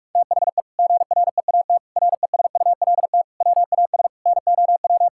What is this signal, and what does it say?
Morse code, heard as a keyed tone, about 41 wpm, 700 Hz. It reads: THE GREAT RESULT WAS NOW
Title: Sentences from Top 1000 Words - Review Part 5 - 45wpm